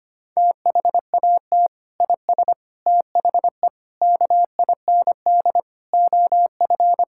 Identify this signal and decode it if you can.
Morse code, heard as a keyed tone, about 25 wpm, 700 Hz. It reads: THAT IS THE KIND OF
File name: practice / 25wpm / Sentences with 3-Word Phrases - 25wpm